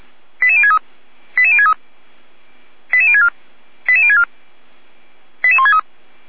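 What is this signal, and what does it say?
Five-tone selective call (ZVEI), heard as quick runs of five tones, tones of about 70 ms. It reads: 88063 88063 88064 88064 88163
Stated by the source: ZVEI five-tone call received by SDR.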